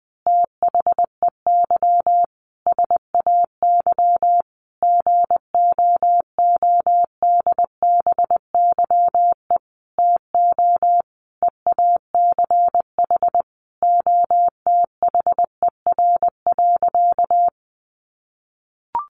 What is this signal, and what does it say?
Morse code, heard as a keyed tone, about 20 wpm, 700 Hz. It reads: THEY SAY GOODBYE TO EACH OTHER.
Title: Additional Easy Phrases 20wpm